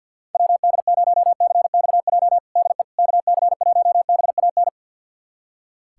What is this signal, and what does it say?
Morse code, heard as a keyed tone, about 46 wpm, 700 Hz. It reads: WD0CXJ DE KC1BAN
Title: USA Callsign DE Callsign - 6 Characters - 50wpm